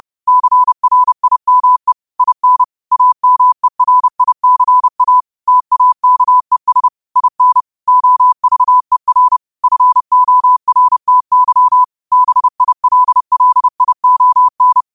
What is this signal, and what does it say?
Morse code, heard as a keyed tone, about 30 wpm, 1000 Hz. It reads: CRIME IN AMERICA TAKES IN OVER FORTY BILLION
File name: qotdfiles / 30wpm